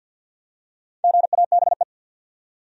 Morse code, heard as a keyed tone, about 48 wpm, 700 Hz. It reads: GABE
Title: USA Common Names - Max 4 Letters - 50wpm